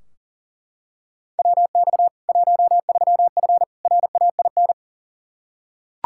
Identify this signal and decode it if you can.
Morse code, heard as a keyed tone, about 40 wpm, 700 Hz. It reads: WX 13F RAIN
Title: K1USN SST QSO Practice 40wpm